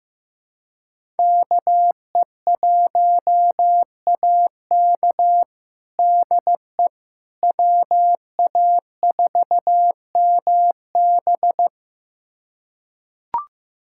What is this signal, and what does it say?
Morse code, heard as a keyed tone, about 15 wpm, 700 Hz. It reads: KE1AK DE WA4MB
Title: USA Callsign DE Callsign - 5 Characters - 15wpm